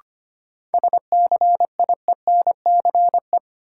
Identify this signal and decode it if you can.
Morse code, heard as a keyed tone, about 25 wpm, 700 Hz. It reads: SCIENCE
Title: Sentences from Top 1000 Words - Review Part 2 - New Word Drill - 25wpm